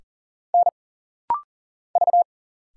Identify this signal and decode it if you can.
Morse code, heard as a keyed tone, about 40 wpm, 700 Hz. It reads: N V
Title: Single Letters - Rapid-Fire 40wpm